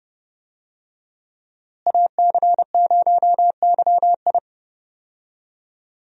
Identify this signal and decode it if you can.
Morse code, heard as a keyed tone, about 32 wpm, 700 Hz. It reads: AC0YI
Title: USA Call Signs - 5 Characters - Repeated 2x - 30wpm